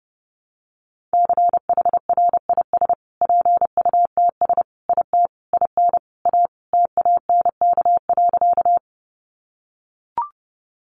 Morse code, heard as a keyed tone, about 30 wpm, 700 Hz. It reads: CHRIS PUTS IT IN A TANK.
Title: Easy Phrases 30wpm